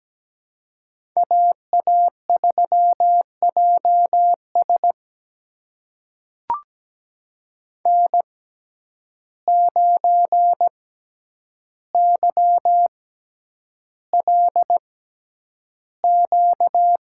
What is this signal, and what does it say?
Morse code, heard as a keyed tone, about 17 wpm, spaced as for 9 wpm, 700 Hz. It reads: AA3JS N9YLQ
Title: USA Call Signs - 5 Characters - 6x Character Spacing - 17wpm